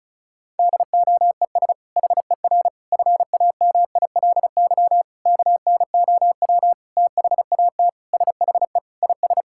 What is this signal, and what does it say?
Morse code, heard as a keyed tone, about 34 wpm, 700 Hz. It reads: DOES HER FAMILY KNOW THAT SHE IS
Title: Sentences from Top 1000 Words - Review Part 6 - 35wpm